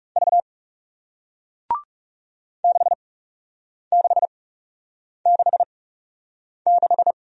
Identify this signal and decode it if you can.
Morse code, heard as a keyed tone, about 40 wpm, spaced as for 8 wpm, 700 Hz. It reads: V 6666